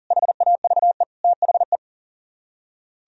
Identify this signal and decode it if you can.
Morse code, heard as a keyed tone, about 50 wpm, 700 Hz. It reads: HAVE THE
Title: Speed-Racing - Sets of 3 Words 40wpm